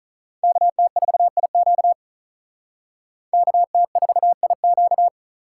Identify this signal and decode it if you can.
Morse code, heard as a keyed tone, about 39 wpm, 700 Hz. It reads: KT4IQ KT4IQ
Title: Speed Racing - USA 2x2 Call Signs 35wpm